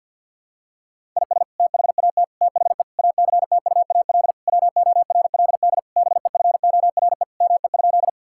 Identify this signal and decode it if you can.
Morse code, heard as a keyed tone, about 50 wpm, 700 Hz. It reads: IS THAT THE ACTUAL WORLD BEFORE ME?
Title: Sentences from Top 1000 Words - Review Part 5 - 50wpm